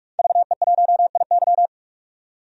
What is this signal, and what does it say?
Morse code, heard as a keyed tone, about 41 wpm, 700 Hz. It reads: VE1IY
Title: DXCC-100 Call Signs - 5 Characters - 45wpm